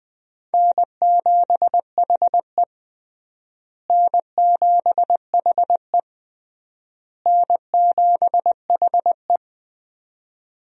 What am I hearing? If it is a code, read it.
Morse code, heard as a keyed tone, about 20 wpm, 700 Hz. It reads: N7HE N7HE N7HE
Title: USA Call Signs - 4 Characters - Repeated 3x - 20wpm